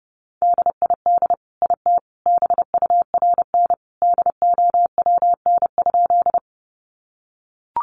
Morse code, heard as a keyed tone, about 30 wpm, 700 Hz. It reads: DID IT BURN DOWN?
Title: Sentences from Top 1300 Words 30wpm